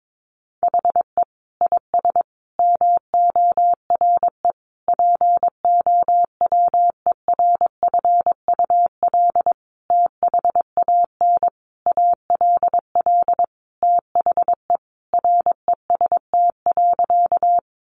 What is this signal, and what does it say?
Morse code, heard as a keyed tone, about 22 wpm, 700 Hz. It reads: HE IS MORE POWERFUL THAN ALL THE REST.
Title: Sentences from Top 1300 Words 22wpm